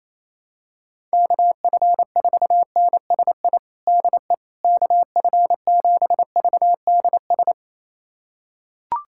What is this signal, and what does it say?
Morse code, heard as a keyed tone, about 28 wpm, 700 Hz. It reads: KF4NSI DE KF7VDS
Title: USA Callsign DE Callsign - 6 Characters - 28wpm